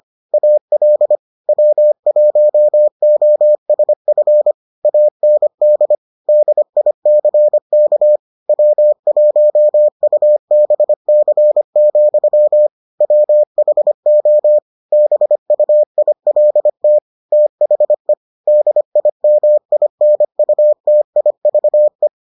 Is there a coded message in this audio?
Yes: Morse code from 0.3 to 22.1 s, heard as a keyed tone, about 25 wpm, 600 Hz. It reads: AL W1OSF AND DICK W1UBC, WHO BUILT THE DIMINUTIVE